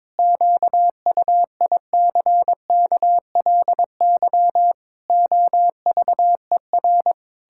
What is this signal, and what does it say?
Morse code, heard as a keyed tone, about 22 wpm, 700 Hz. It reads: QUICKLY OVER
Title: Pangram Challenge v2 - 22wpm